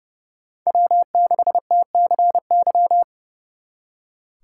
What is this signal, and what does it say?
Morse code, heard as a keyed tone, about 36 wpm, 700 Hz. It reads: W6TCY
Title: Speed Racing - USA 1x3 Call Signs 30wpm